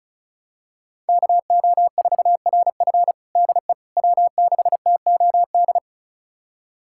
Morse code, heard as a keyed tone, about 36 wpm, 700 Hz. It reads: KO4RF DE W6TOD
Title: USA Callsign DE Callsign - 5 Characters - 35wpm